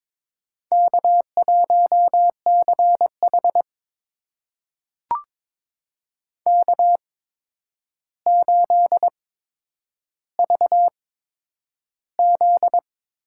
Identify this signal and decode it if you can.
Morse code, heard as a keyed tone, about 22 wpm, spaced as for 9 wpm, 700 Hz. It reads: K1CH K8VZ
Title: USA Call Signs - 4 Characters - 8x Character Spacing - 22wpm